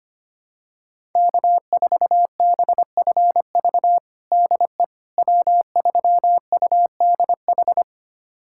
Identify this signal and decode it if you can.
Morse code, heard as a keyed tone, about 25 wpm, 700 Hz. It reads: K4BFV DE W3UDH
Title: USA Callsign DE Callsign - 5 Characters - 25wpm